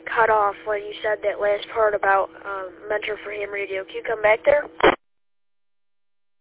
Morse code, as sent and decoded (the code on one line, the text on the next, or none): none